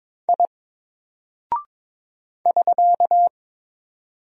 Morse code, heard as a keyed tone, about 22 wpm, 700 Hz. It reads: I <SK>
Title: Single Character (Letter:Number:Punctuation:Prosign) 22wpm